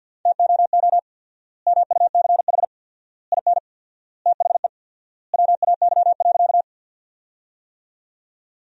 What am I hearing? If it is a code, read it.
Morse code, heard as a keyed tone, about 52 wpm, 700 Hz. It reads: TOO MUCH IN THE WAY.
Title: 4x Word-Spacing - Sentences from Top 200 Words - 50wpm